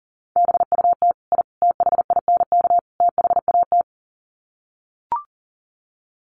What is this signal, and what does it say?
Morse code, heard as a keyed tone, about 40 wpm, 700 Hz. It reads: BUT I THINK THAT
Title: Sets of 4 Words 40wpm